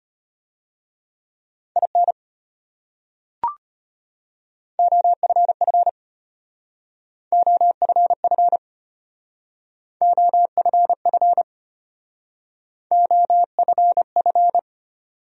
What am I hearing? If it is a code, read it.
Morse code, heard as a keyed tone, about 34 wpm, 700 Hz. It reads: IN OFF OFF OFF OFF